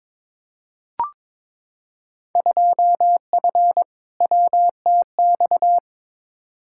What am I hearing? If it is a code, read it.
Morse code, heard as a keyed tone, about 22 wpm, 700 Hz. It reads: 2F WTX
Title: Field Day Exchanges - 22wpm